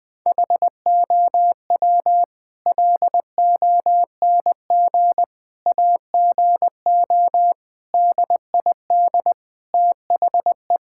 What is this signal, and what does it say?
Morse code, heard as a keyed tone, about 20 wpm, 700 Hz. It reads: HOW LONG AGO DID THE
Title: Sentences from Top 1000 Words - Review Part 6 - 20wpm